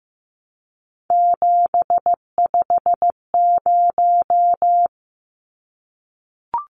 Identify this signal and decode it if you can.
Morse code, heard as a keyed tone, about 15 wpm, 700 Hz. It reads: 750
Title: Numbers - 3-Digit 15wpm